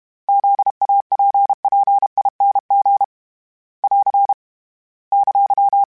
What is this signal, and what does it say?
Morse code, heard as a keyed tone, about 32 wpm, 800 Hz. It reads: ZAPPING <AR> !
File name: Special INGs-mp3s / Special INGs-mp3s / Sngl Ing @30